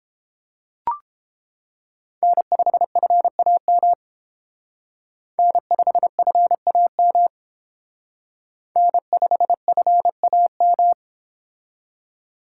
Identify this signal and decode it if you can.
Morse code, heard as a keyed tone, about 30 wpm, 700 Hz. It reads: N5FAM N5FAM N5FAM